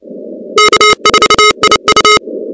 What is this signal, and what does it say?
Morse code, heard as a keyed tone, about 29 wpm, 1270 Hz, with noise. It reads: K4IU
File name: contests / 2019-cw-wpx-kt5j-k5pi- -2330 0-16